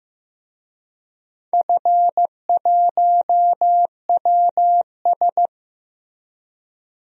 Morse code, heard as a keyed tone, about 15 wpm, 700 Hz. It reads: F1WS
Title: DXCC-100 Call Signs - 4 Characters - Repeated 2x - 15wpm